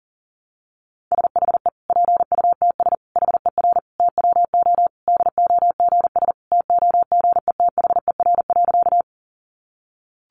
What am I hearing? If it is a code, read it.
Morse code, heard as a keyed tone, about 40 wpm, 700 Hz. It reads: SHE PUTS HER TWO DOGS TOGETHER.
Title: Easy Phrases 40wpm